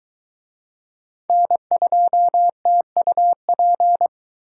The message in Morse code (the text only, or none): N2TUP